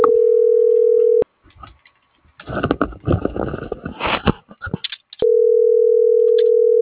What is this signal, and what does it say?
Ringback tone (North America), 440 + 480 Hz.
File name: CENTRAL SOTOVA / out carlos.sotova